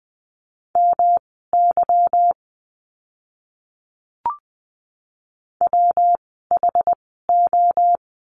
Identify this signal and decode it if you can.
Morse code, heard as a keyed tone, about 20 wpm, spaced as for 15 wpm, 700 Hz. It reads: MY WHO
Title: 146 Lesson 24f - Letter y - Words 20wpm